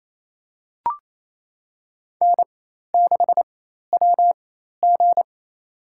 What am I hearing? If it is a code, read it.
Morse code, heard as a keyed tone, about 28 wpm, spaced as for 13 wpm, 700 Hz. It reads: N6WG
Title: USA Call Signs - 4 Characters - 4x Character Spacing - 28wpm